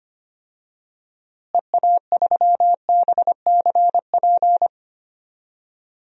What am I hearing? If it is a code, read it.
Morse code, heard as a keyed tone, about 25 wpm, 700 Hz. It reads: EA3BCP
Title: DXCC-100 Call Signs - 6 Characters - Repeated 3x - 25wpm